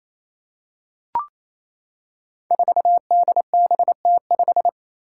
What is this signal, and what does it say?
Morse code, heard as a keyed tone, about 28 wpm, 700 Hz. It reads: 4DBT5
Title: Five Challenging Letter-Number Combinations 28wpm